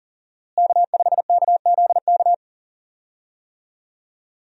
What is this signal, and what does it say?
Morse code, heard as a keyed tone, about 40 wpm, 700 Hz. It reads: K5KZK
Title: USA Call Signs - 5 Characters - Repeated 3x - 40wpm